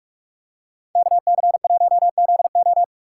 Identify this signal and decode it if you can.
Morse code, heard as a keyed tone, about 43 wpm, 700 Hz. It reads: KC1ZO